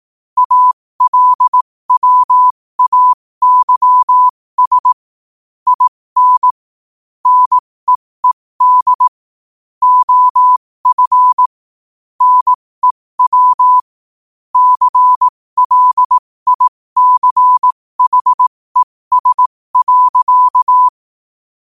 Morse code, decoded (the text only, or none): ALWAYS IN NEED OF NEW CLICHES.